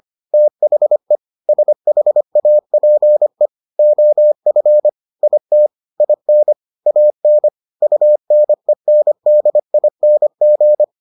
Morse code, heard as a keyed tone, about 25 wpm, 600 Hz. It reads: THE SHAPE OF IT IN AN UNENDING